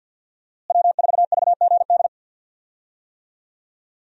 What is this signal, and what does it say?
Morse code, heard as a keyed tone, about 45 wpm, 700 Hz. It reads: W4VGD
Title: USA Call Signs - 5 Characters - Repeated 3x - 50wpm